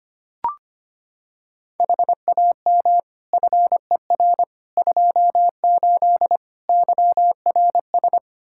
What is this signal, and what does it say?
Morse code, heard as a keyed tone, about 25 wpm, 700 Hz. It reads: HAM FER 28 YRS
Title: Practice QSO - Exchange 3 - More Info - 25wpm